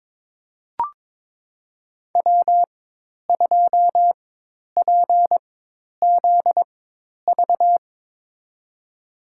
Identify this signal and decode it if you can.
Morse code, heard as a keyed tone, about 22 wpm, spaced as for 10 wpm, 700 Hz. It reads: W2PZV
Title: USA Call Signs - 5 Characters - 4x Character Spacing - 22wpm